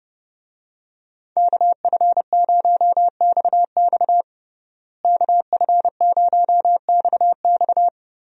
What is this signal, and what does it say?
Morse code, heard as a keyed tone, about 30 wpm, 700 Hz. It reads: KF0XX KF0XX
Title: USA Call Signs - 5 Characters - Repeated 2x - 30wpm